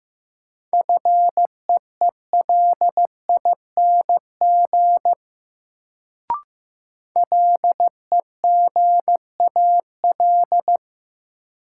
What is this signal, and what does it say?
Morse code, heard as a keyed tone, about 15 wpm, 700 Hz. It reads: FEELING LEGAL
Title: Sentences from Top 1200 Words - New Word Drill Alt - 15wpm